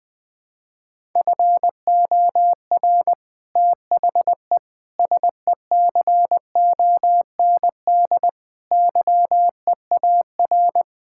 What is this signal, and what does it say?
Morse code, heard as a keyed tone, about 20 wpm, 700 Hz. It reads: FOR THE SECOND YEAR
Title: Sets of 8 Words - 20wpm